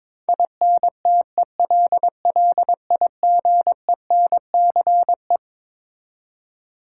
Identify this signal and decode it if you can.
Morse code, heard as a keyed tone, about 22 wpm, 700 Hz. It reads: INTELLIGENCE
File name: Twelve Letter Words - 22wpm